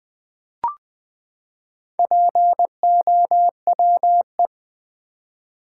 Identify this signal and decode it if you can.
Morse code, heard as a keyed tone, about 20 wpm, 700 Hz. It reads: POWE
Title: Build A Word - 4 Letters At A Time - 8-Letter Words - 20wpm